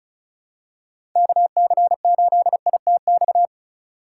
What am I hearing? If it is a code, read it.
Morse code, heard as a keyed tone, about 36 wpm, 700 Hz. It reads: KC8ITX
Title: USA Call Signs - 6 Characters - 2x Character Spacing - 35wpm